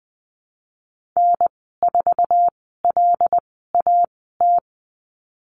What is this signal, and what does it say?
Morse code, heard as a keyed tone, about 20 wpm, spaced as for 15 wpm, 700 Hz. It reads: N4LAT
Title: 128 Lesson 21g - Number 6 - Call Signs 20wpm